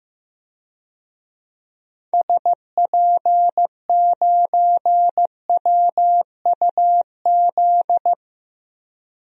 Morse code, heard as a keyed tone, about 15 wpm, 700 Hz. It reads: SP9WUZ
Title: DXCC-100 Call Signs - 6 Characters - 15wpm